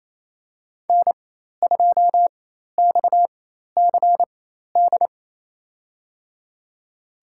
Morse code, heard as a keyed tone, about 28 wpm, spaced as for 13 wpm, 700 Hz. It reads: N2XCD
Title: USA Call Signs - 5 Characters - 4x Character Spacing - 28wpm